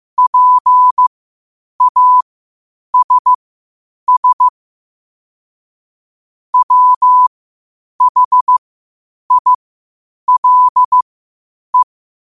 Morse code, heard as a keyed tone, about 15 wpm, spaced as for 8 wpm, 1000 Hz. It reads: PASS WHILE
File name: qotdfiles / 8wpm